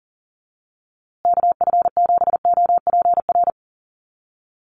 Morse code, heard as a keyed tone, about 40 wpm, 700 Hz. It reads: KF7OPR
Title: USA Call Signs Repeated 2x 40wpm